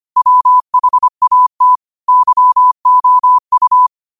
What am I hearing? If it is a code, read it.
Morse code, heard as a keyed tone, about 25 wpm, 1000 Hz. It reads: WHAT YOU
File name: qotdfiles / 25wpm